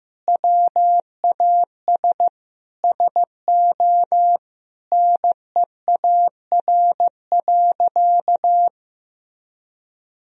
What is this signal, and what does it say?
Morse code, heard as a keyed tone, about 15 wpm, 700 Hz. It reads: WAS SO NEAR.
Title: Sentences from Top 300 Words and Abbreviations 15wpm